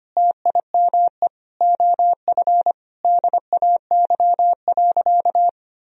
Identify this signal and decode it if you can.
Morse code, heard as a keyed tone, about 25 wpm, 700 Hz. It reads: TIME OF DAY.
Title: Sentences from Top 1000 Words - Review Part 4 - 25wpm